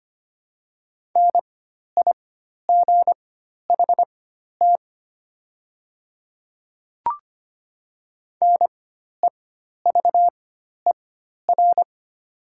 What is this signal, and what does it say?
Morse code, heard as a keyed tone, about 25 wpm, spaced as for 12 wpm, 700 Hz. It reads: NIGHT NEVER
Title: 4x Character Spacing - Top 300 Words - 25wpm